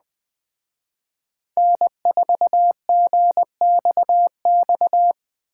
Morse code, heard as a keyed tone, about 21 wpm, 700 Hz. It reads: N4GXX